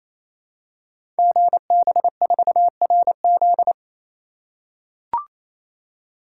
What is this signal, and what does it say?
Morse code, heard as a keyed tone, about 27 wpm, 700 Hz. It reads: GB4RZ
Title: Code Groups - 5 Letter:Numbers - 28wpm